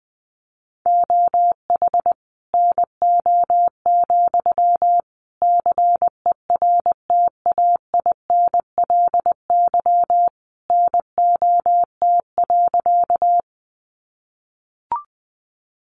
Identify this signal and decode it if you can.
Morse code, heard as a keyed tone, about 20 wpm, 700 Hz. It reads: OH NO, CERTAINLY NOT.
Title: Sentences from Top 900 Words 20wpm